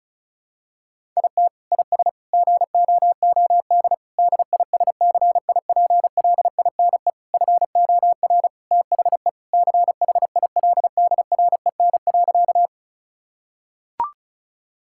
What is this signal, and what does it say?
Morse code, heard as a keyed tone, about 36 wpm, 700 Hz. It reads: IT IS GOOD DISCIPLINE FOR THE CHILDREN.
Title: Sentences from Top 1000 Words - Review Part 6 - 35wpm